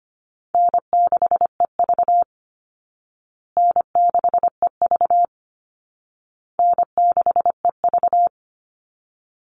Morse code, heard as a keyed tone, about 25 wpm, 700 Hz. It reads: N6EV N6EV N6EV
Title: Easy Call Signs Repeated 3x - 25wpm